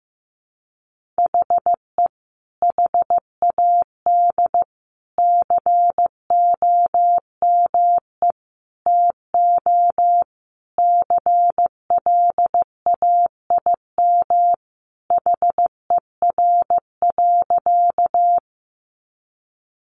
Morse code, heard as a keyed tone, about 15 wpm, 700 Hz. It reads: HE HAD COME TO CLAIM HER.
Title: Sentences from Top 1500 Words 15wpm